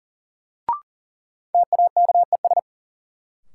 Morse code, heard as a keyed tone, about 40 wpm, 700 Hz. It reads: TAKES
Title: Sets of 5 Words Encore - New Word Drill - 40wpm